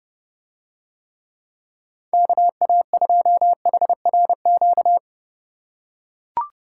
Morse code, heard as a keyed tone, about 30 wpm, 700 Hz. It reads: KA2HRQ